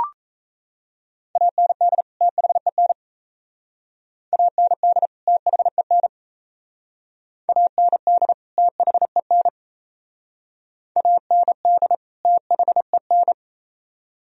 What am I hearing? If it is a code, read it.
Morse code, heard as a keyed tone, about 37 wpm, 700 Hz. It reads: AND THEN AND THEN AND THEN AND THEN